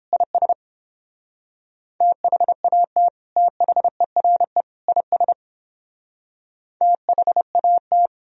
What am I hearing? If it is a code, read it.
Morse code, heard as a keyed tone, about 31 wpm, 700 Hz. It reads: IS THAT THERE IS THAT